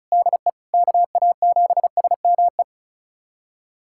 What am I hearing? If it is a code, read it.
Morse code, heard as a keyed tone, about 34 wpm, 700 Hz. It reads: DE KA7SME
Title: USA Callsign DE Callsign - 6 Characters - 35wpm